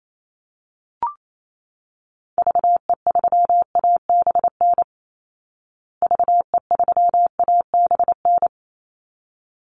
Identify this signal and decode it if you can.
Morse code, heard as a keyed tone, about 28 wpm, 700 Hz. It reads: VE3ABN VE3ABN
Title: Call Signs Repeated 2x - 28wpm